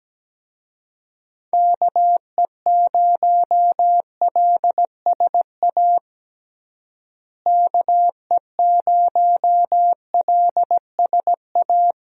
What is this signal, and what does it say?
Morse code, heard as a keyed tone, about 17 wpm, 700 Hz. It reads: KE0LSA KE0LSA
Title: USA Call Signs - 6 Characters - Repeated 2x - 17wpm